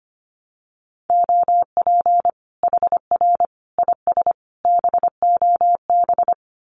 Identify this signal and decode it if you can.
Morse code, heard as a keyed tone, about 25 wpm, 700 Hz. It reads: OP HR IS BOB
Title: Common QSO Phrases 25wpm